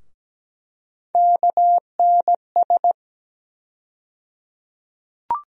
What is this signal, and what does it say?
Morse code, heard as a keyed tone, about 17 wpm, 700 Hz. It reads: KNS
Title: USA Call Sign Postfixes - 3 Letters 17wpm